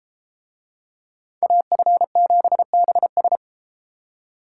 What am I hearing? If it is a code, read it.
Morse code, heard as a keyed tone, about 32 wpm, 700 Hz. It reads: AF7BS